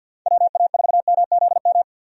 Morse code, heard as a keyed tone, about 51 wpm, 700 Hz. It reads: WA4MZM
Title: USA Call Signs - 6 Characters - 50wpm